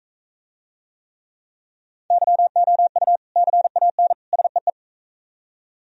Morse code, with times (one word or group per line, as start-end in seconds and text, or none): 2.1-3.2 YOU
3.4-4.1 CAN
4.3-4.7 SEE